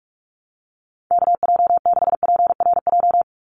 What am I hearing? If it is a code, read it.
Morse code, heard as a keyed tone, about 44 wpm, 700 Hz. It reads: KJ6PRJ
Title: USA Call Signs 45wpm